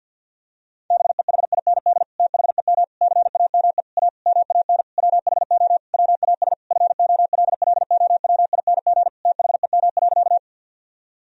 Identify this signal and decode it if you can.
Morse code, heard as a keyed tone, about 50 wpm, 700 Hz. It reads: BEHIND THEM CAME A MAN WHO WAS FOLLOWING THEM.
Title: Sentences from Top 1000 Words - Review Part 4 - 50wpm